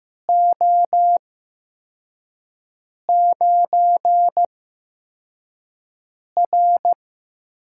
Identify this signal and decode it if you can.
Morse code, heard as a keyed tone, about 15 wpm, spaced as for 4 wpm, 700 Hz. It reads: O9R
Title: USA Call Signs - 4 Characters - 8x Character Spacing - 15wpm